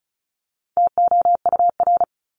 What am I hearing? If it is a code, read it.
Morse code, heard as a keyed tone, about 34 wpm, 700 Hz. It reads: TOUR
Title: Top 1251-1500 Words 35wpm